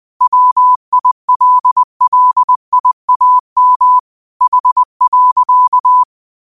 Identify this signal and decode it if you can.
Morse code, heard as a keyed tone, about 20 wpm, 1000 Hz. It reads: WILLIAM H.
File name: qotdfiles / 20wpm